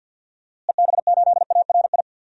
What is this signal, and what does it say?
Morse code, heard as a keyed tone, about 46 wpm, 700 Hz. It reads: EB8ARI
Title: DXCC-100 Call Signs - 6 Characters - Repeated 3x - 50wpm